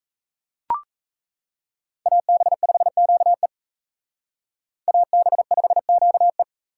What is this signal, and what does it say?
Morse code, heard as a keyed tone, about 35 wpm, 700 Hz. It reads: AB5QE AB5QE